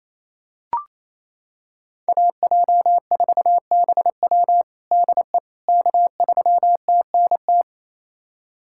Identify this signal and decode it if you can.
Morse code, heard as a keyed tone, about 28 wpm, 700 Hz. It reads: AJ4BW DE K3TNT
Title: USA Callsign DE Callsign - 5 Characters - 28wpm